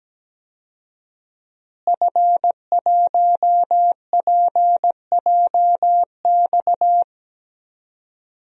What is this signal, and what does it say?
Morse code, heard as a keyed tone, about 17 wpm, 700 Hz. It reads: F1PJX
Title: DXCC-100 Call Signs - 5 Characters - Repeated 3x - 17wpm